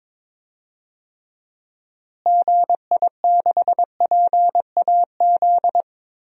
Morse code, heard as a keyed tone, about 22 wpm, 700 Hz. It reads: GI6PAZ
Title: DXCC-100 Call Signs - 6 Characters - Repeated 3x - 22wpm